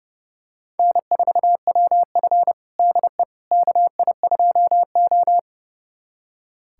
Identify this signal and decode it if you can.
Morse code, heard as a keyed tone, about 32 wpm, 700 Hz. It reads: N4WF DE KI2O
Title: USA Callsign DE Callsign - 4 Characters - 30wpm